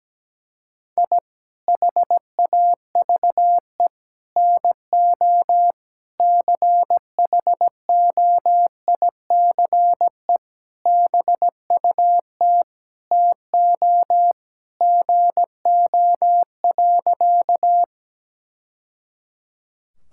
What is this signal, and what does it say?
Morse code, heard as a keyed tone, about 17 wpm, 700 Hz. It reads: I HAVE NO CHOICE BUT TO GO.
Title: Sentences from Top 1000 Words - Review Part 6 - 17wpm